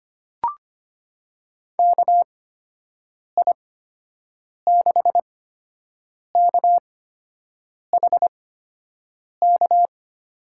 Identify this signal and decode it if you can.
Morse code, heard as a keyed tone, about 25 wpm, spaced as for 7 wpm, 700 Hz. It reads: KI6KHK
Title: USA Call Signs - 6 Characters - 8x Character Spacing - 25wpm